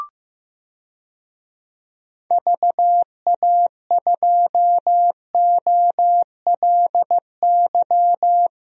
Morse code, heard as a keyed tone, about 15 wpm, 700 Hz. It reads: VA2OLY
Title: DXCC-100 Call Signs - 6 Characters - Repeated 2x - 15wpm